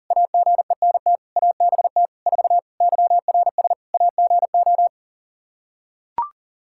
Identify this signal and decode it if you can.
Morse code, heard as a keyed tone, about 40 wpm, 700 Hz. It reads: AGENT ABT 4 YRS AGO
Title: K1USN SST QSO Practice 40wpm